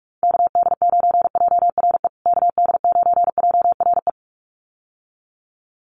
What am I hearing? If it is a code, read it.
Morse code, heard as a keyed tone, about 44 wpm, 700 Hz. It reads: KD9JRE KD9JRE
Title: USA Call Signs Repeated 2x 45wpm